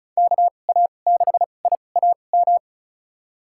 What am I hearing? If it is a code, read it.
Morse code, heard as a keyed tone, about 36 wpm, spaced as for 25 wpm, 700 Hz. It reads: KA6IAM
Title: USA Call Signs - 6 Characters - 2x Character Spacing - 35wpm